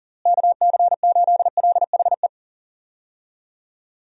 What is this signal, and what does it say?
Morse code, heard as a keyed tone, about 40 wpm, 700 Hz. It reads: KC8LHE
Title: USA Call Signs - 6 Characters - Repeated 2x - 40wpm